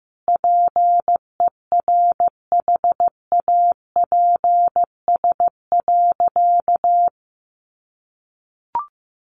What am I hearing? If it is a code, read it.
Morse code, heard as a keyed tone, about 15 wpm, 700 Hz. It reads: PERHAPS.
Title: Sentences from Top 1400 Words 15wpm